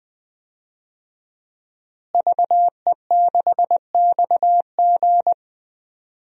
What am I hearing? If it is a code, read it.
Morse code, heard as a keyed tone, about 21 wpm, 700 Hz. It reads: VE6XG
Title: DXCC-100 Call Signs - 5 Characters - Repeated 2x - 20wpm